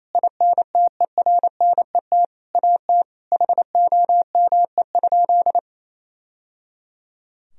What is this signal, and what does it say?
Morse code, heard as a keyed tone, about 28 wpm, 700 Hz. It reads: INTERNET AT HOME?
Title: Sentences from Top 1000 Words - Review Part 4 - 28wpm